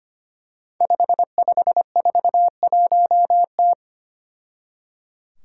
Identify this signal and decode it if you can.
Morse code, heard as a keyed tone, about 25 wpm, 700 Hz. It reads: 5541T
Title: Five Challenging Letter-Number Combinations 25wpm